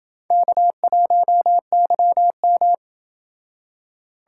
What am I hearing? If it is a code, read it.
Morse code, heard as a keyed tone, about 28 wpm, 700 Hz. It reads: K1YM